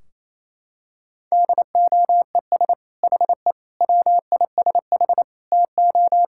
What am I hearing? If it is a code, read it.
Morse code, heard as a keyed tone, about 28 wpm, 700 Hz. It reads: DOES HE WISH TO
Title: Sentences from Top 1600 Words 28wpm